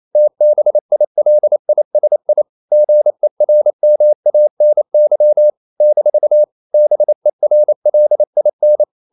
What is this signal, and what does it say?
Morse code, heard as a keyed tone, about 28 wpm, 600 Hz. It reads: TBILISI GERMANY - BERLIN
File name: CW / Practice / NationalCapitals28